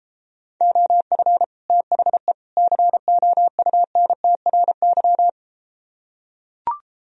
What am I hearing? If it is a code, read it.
Morse code, heard as a keyed tone, about 33 wpm, 700 Hz. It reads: OF THE COUNTRY